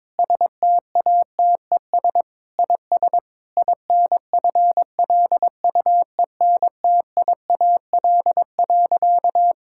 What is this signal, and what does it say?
Morse code, heard as a keyed tone, about 22 wpm, 700 Hz. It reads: STATES IS INFLUENTIAL.
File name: practice / 22wpm / Sentences with 4-Word Phrases - 22wpm